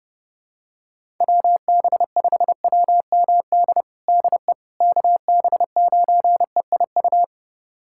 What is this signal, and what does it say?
Morse code, heard as a keyed tone, about 30 wpm, 700 Hz. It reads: WB5WMD DE KB9EIU
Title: USA Callsign DE Callsign - 6 Characters - 30wpm